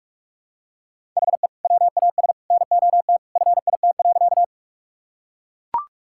Morse code, heard as a keyed tone, about 44 wpm, 700 Hz. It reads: HE WAS NOT FIT.
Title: Sentences from Top 1000 Words - Review Part 5 - 45wpm